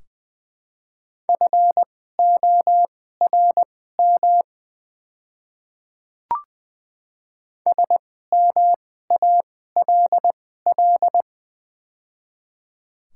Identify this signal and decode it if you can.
Morse code, heard as a keyed tone, about 20 wpm, spaced as for 14 wpm, 700 Hz. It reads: FORM SMALL
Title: 2x Character Spacing - Top 300 Words - 20wpm